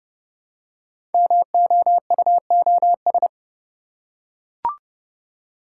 Morse code, heard as a keyed tone, about 30 wpm, 700 Hz. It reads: MOUOS
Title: Code Groups - 5 Letters - 30wpm